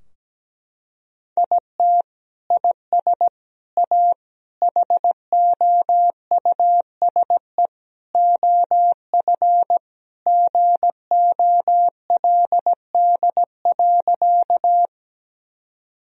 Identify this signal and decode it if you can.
Morse code, heard as a keyed tone, about 17 wpm, 700 Hz. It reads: IT IS A HOUSE OF GOLD.
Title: Sentences from Top 1000 Words - Review Part 3 - 17wpm